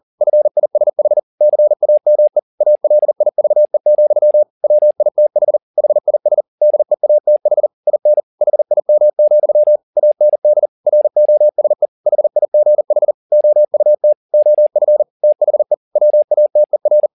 Morse code, heard as a keyed tone, about 40 wpm, 600 Hz. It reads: FISH CAME ALIVE, WITH HIS DEATH IN HIM, AND ROSE HIGH OUT OF THE WATER
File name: the old man and the sea 2 40wpm